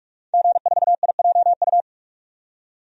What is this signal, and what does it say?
Morse code, heard as a keyed tone, about 46 wpm, 700 Hz. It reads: G4IJU
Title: DXCC-100 Call Signs - 5 Characters - 45wpm